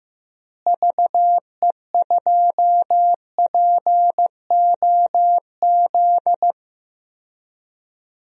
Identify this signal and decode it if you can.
Morse code, heard as a keyed tone, about 15 wpm, 700 Hz. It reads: VE2POZ
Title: DXCC-100 Call Signs - 6 Characters - Repeated 2x - 15wpm